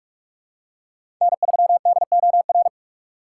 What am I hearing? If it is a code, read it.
Morse code, heard as a keyed tone, about 46 wpm, 700 Hz. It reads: N3DOR
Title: USA Call Signs - 5 Characters - 4x Character Spacing - 45wpm